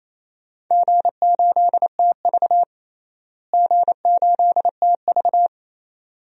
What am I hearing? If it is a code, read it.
Morse code, heard as a keyed tone, about 28 wpm, 700 Hz. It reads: G8TV G8TV
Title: DXCC-100 Call Signs - 4 Characters - Repeated 2x - 28wpm